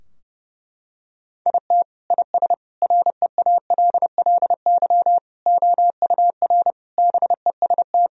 Morse code, heard as a keyed tone, about 30 wpm, 700 Hz. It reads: IT IS REALLY OUR BEST
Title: Sentences from Top 1000 Words - Review Part 2 - 30wpm